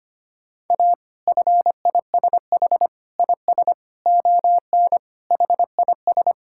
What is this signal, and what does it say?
Morse code, heard as a keyed tone, about 25 wpm, 700 Hz. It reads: A FISH IS ON HIS
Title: Sentences from Top 1600 Words 25wpm